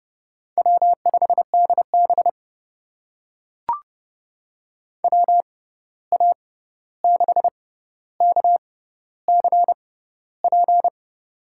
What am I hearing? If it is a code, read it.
Morse code, heard as a keyed tone, about 32 wpm, spaced as for 14 wpm, 700 Hz. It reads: W5DB WA6KCP
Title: USA Call Signs - 6x Character Spacing - 30wpm